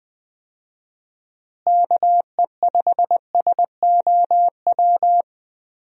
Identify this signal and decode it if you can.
Morse code, heard as a keyed tone, about 19 wpm, 700 Hz. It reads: KE5SOW
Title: USA Call Signs - 6 Characters - Repeated 3x - 20wpm